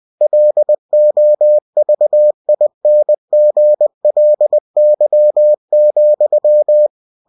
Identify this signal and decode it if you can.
Morse code, heard as a keyed tone, about 20 wpm, 600 Hz. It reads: LOVINGLY,
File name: cw-books / winnie-the-pooh / winnie-20wpm-page-0020